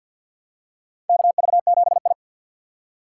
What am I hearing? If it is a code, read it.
Morse code, heard as a keyed tone, about 47 wpm, 700 Hz. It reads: KV7I
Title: USA Call Signs - 3x Character Spacing - 50wpm